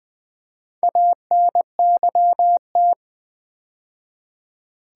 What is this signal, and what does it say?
Morse code, heard as a keyed tone, about 20 wpm, 700 Hz. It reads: ANYT
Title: Build A Word - 2 Letters At A Time - 8-Letter Words - 20wpm